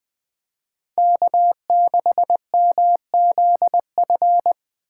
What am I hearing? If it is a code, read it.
Morse code, heard as a keyed tone, about 24 wpm, 700 Hz. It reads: K6MZF
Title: Speed Racing - USA 1x3 Call Signs 20wpm